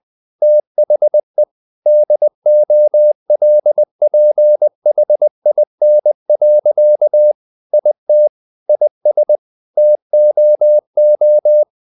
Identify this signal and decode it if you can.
Morse code, heard as a keyed tone, about 20 wpm, 600 Hz. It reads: THE DOLPHIN. IT IS TOO
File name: the old man and the sea part 2 20wpm